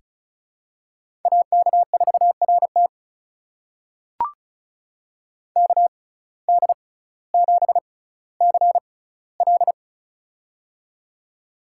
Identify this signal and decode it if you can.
Morse code, heard as a keyed tone, about 36 wpm, spaced as for 18 wpm, 700 Hz. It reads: AK4RT KD7CL
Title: USA Call Signs - 5 Characters - 6x Character Spacing - 35wpm